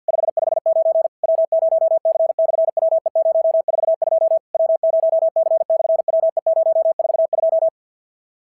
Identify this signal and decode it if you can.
Morse code, heard as a keyed tone, about 51 wpm, 690 Hz. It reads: 559 W0C/WE042 W0C/WE042
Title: SOTA QSO Practice 50wpm